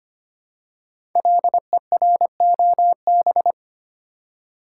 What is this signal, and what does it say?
Morse code, heard as a keyed tone, about 25 wpm, 700 Hz. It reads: LEROB
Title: Code Groups - 5 Letters - 25wpm